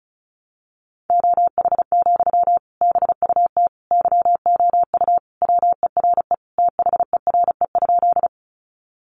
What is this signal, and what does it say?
Morse code, heard as a keyed tone, about 34 wpm, 700 Hz. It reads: OH, BUT YOU WERE THERE?
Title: Sentences from Top 500 Words 35wpm